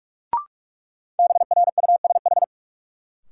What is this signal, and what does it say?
Morse code, heard as a keyed tone, about 42 wpm, 700 Hz. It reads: BRUSH
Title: Sentences from Top 1100 Words - New Word Drill Alt - 45wpm